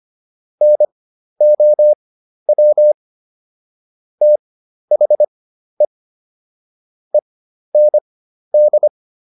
Morse code, heard as a keyed tone, about 25 wpm, spaced as for 12 wpm, 600 Hz. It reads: NOW THE END